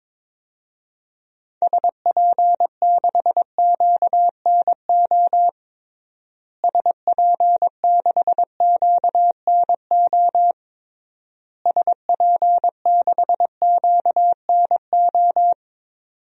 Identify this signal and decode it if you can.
Morse code, heard as a keyed tone, about 22 wpm, 700 Hz. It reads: SP6QNO SP6QNO SP6QNO
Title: DXCC-100 Call Signs - 6 Characters - Repeated 3x - 22wpm